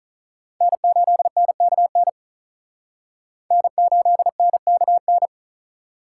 Morse code, heard as a keyed tone, about 43 wpm, 700 Hz. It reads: N8NKN N8NKN
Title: Speed Racing - USA 1x3 Call Signs 35wpm